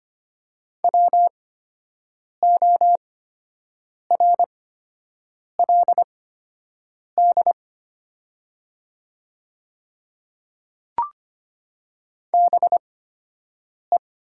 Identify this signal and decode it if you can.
Morse code, heard as a keyed tone, about 25 wpm, spaced as for 7 wpm, 700 Hz. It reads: WORLD BE
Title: 8x Character Spacing - Top 200 Words - 25wpm